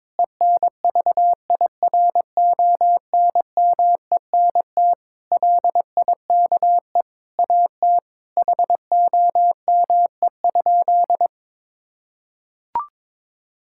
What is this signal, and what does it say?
Morse code, heard as a keyed tone, about 22 wpm, 700 Hz. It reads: ENVIRONMENT LIKE AT HOME?
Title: Sentences from Top 1000 Words - Review Part 5 - 22wpm